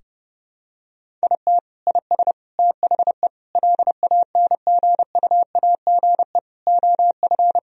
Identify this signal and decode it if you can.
Morse code, heard as a keyed tone, about 32 wpm, 700 Hz. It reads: IT IS THE LANGUAGE OF
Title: Sentences from Top 1000 Words - Review Part 3 - 30wpm